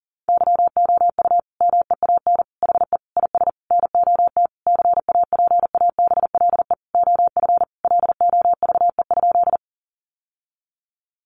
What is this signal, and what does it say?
Morse code, heard as a keyed tone, about 40 wpm, 700 Hz. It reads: YOU MEAN HE IS NOT CAPABLE OF LOVE?
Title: Sentences from Top 1200 Words 40wpm